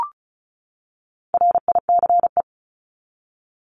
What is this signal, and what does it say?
Morse code, heard as a keyed tone, about 34 wpm, 700 Hz. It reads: RICE
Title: Top 1501-1750 Words 35wpm